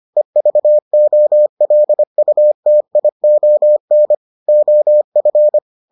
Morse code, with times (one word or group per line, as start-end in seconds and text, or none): 0.2-4.1 EVOLUTION
4.5-5.6 OF